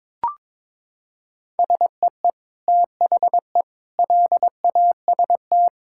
Morse code, heard as a keyed tone, about 22 wpm, 700 Hz. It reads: SEE THE LAST
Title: Sentences from Top 1000 Words - Review Part 5 - 22wpm